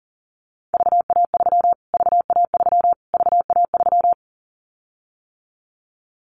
Morse code, heard as a keyed tone, about 40 wpm, 700 Hz. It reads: VA3 VA3 VA3
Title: Callsign Prefixes for Top 25 DXCC - Repeated 3x - 40wpm